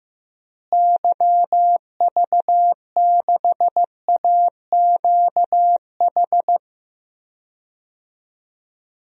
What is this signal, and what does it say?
Morse code, heard as a keyed tone, about 15 wpm, 700 Hz. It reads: YV6AQH
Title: DXCC-100 Call Signs - 6 Characters - Repeated 2x - 15wpm